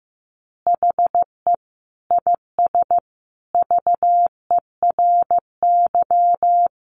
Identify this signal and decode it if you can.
Morse code, heard as a keyed tone, about 15 wpm, 700 Hz. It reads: HE IS VERY
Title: Easy Phrases 15wpm